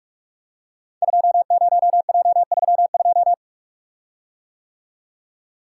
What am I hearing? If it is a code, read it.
Morse code, heard as a keyed tone, about 44 wpm, 700 Hz. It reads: 20J32
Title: Code Groups - 5 Letter:Numbers - 45wpm